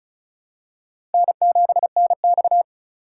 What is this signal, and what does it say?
Morse code, heard as a keyed tone, about 34 wpm, 700 Hz. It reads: N7NX
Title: USA Call Signs - 3x Character Spacing - 35wpm